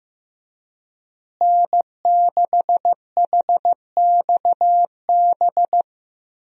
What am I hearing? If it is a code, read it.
Morse code, heard as a keyed tone, about 15 wpm, 700 Hz. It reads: N6HXB
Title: USA Call Signs - 5 Characters - Repeated 3x - 15wpm